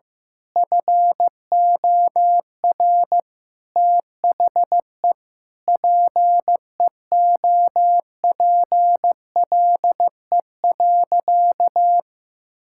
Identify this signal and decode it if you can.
Morse code, heard as a keyed tone, about 15 wpm, 700 Hz. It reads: FOR THE PEOPLE.
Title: Sentences from Top 1000 Words - Review Part 5 - 15wpm